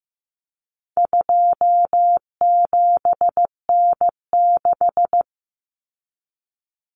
Morse code, heard as a keyed tone, about 15 wpm, 700 Hz. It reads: 27N6
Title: Common Cut Numbers - 4-Digit 15wpm